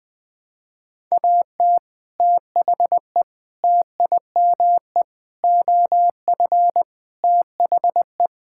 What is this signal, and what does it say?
Morse code, heard as a keyed tone, about 20 wpm, 700 Hz. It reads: AT THE TIME OF THE
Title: Sentences with 4-Word Phrases - 20wpm